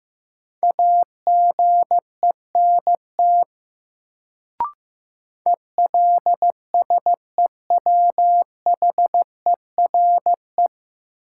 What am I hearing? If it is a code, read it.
Morse code, heard as a keyed tone, about 15 wpm, 700 Hz. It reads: AGENT ELSEWHERE
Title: Sentences from Top 1600 Words - New Word Drill Alt - 15wpm